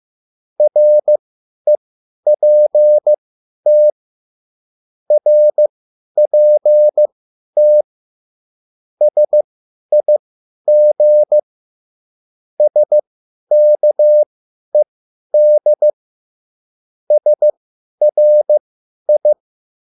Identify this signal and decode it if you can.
Morse code, heard as a keyed tone, about 15 wpm, spaced as for 10 wpm, 600 Hz. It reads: REPT RPT SIG SKED SRI
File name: QSL / CW lesson